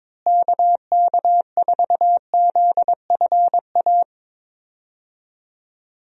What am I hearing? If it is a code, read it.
Morse code, heard as a keyed tone, about 21 wpm, 700 Hz. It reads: KK4ZFA
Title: USA Call Signs - 6 Characters - Repeated 3x - 22wpm